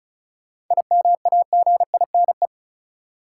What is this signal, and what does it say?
Morse code, heard as a keyed tone, about 36 wpm, 700 Hz. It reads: IMAGINE
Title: Sentences from Top 1400 Words - New Word Drill Alt - 35wpm